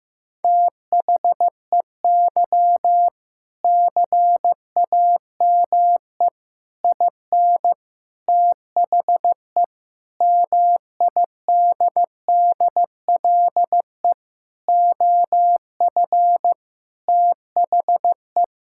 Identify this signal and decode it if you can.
Morse code, heard as a keyed tone, about 15 wpm, 700 Hz. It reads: THEY CAME IN THE MIDDLE OF THE
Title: Sentences from Top 1000 Words - Review Part 6 - 15wpm